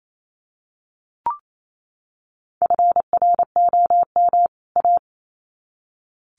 Morse code, heard as a keyed tone, about 28 wpm, 700 Hz. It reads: FROM A
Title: Sets of 2 Words 28wpm